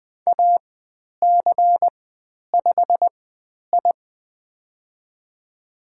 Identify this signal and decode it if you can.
Morse code, heard as a keyed tone, about 20 wpm, spaced as for 10 wpm, 700 Hz. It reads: AC5I
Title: 095 Lesson 16g - Letter c - Call Signs 20wpm